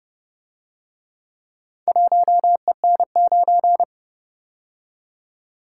Morse code, heard as a keyed tone, about 31 wpm, 700 Hz. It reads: 1EN9